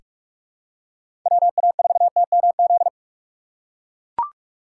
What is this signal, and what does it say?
Morse code, heard as a keyed tone, about 41 wpm, 700 Hz. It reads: WA4TMZ
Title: USA Call Signs - 6 Characters - 3x Character Spacing - 45wpm